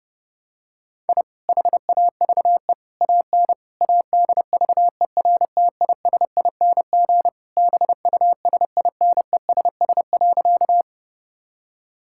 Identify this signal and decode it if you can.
Morse code, heard as a keyed tone, about 32 wpm, 700 Hz. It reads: I HAVE AN ADVERTISING BUSINESS.
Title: Sentences from Top 1000 Words - Review Part 6 - 30wpm